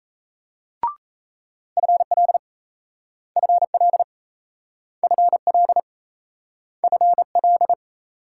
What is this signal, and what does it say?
Morse code, heard as a keyed tone, about 38 wpm, 700 Hz. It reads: FL FL FL FL